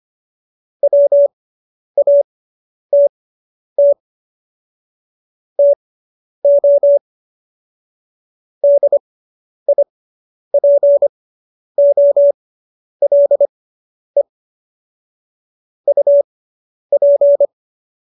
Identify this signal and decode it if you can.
Morse code, heard as a keyed tone, about 25 wpm, spaced as for 10 wpm, 570 Hz. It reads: WATT TO DIPOLE UP
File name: QSO220 10